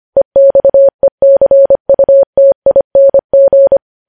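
Morse code, heard as a keyed tone, about 25 wpm, 550 Hz. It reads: EXECUTING
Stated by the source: Morse code news headlines sent at 25.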